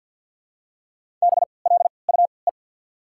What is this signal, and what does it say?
Morse code, heard as a keyed tone, about 53 wpm, spaced as for 29 wpm, 700 Hz. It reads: BLUE
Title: 3x Character Spacing - Top 500 Words - 50wpm